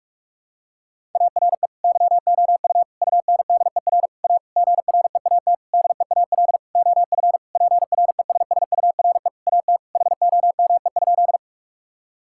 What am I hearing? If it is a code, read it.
Morse code, heard as a keyed tone, about 46 wpm, 700 Hz. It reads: ARE YOU UNDER A GREAT DEAL OF PRESSURE AT HOME?
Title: Sentences from Top 1000 Words - Review Part 6 - 45wpm